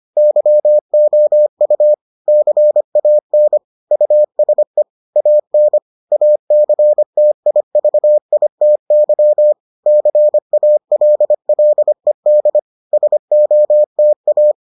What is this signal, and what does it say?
Morse code, heard as a keyed tone, about 25 wpm, 600 Hz. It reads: YOU CAN USE AN ACTIVITY CALLED SOTA